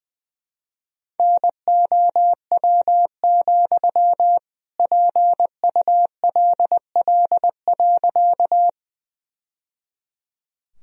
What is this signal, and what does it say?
Morse code, heard as a keyed tone, about 20 wpm, 700 Hz. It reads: NOW, PULL.
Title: Sentences from Top 1000 Words - Review Part 2 - 20wpm